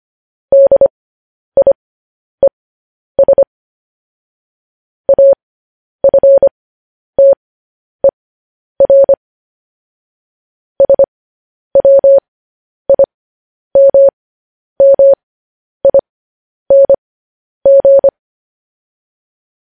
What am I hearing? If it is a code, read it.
Morse code, heard as a keyed tone, about 25 wpm, spaced as for 10 wpm, 550 Hz. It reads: DIES AFTER SWIMMING
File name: podcast / News headlines-1x-25x10